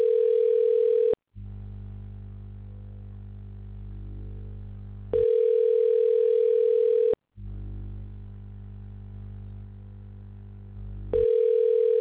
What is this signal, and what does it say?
Ringback tone (North America), 440 + 480 Hz.